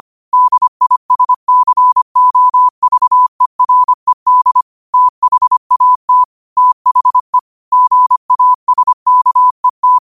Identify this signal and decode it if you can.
Morse code, heard as a keyed tone, about 25 wpm, 1000 Hz. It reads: DISCOVERED THAT THE GASKET
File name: qotdfiles / 25wpm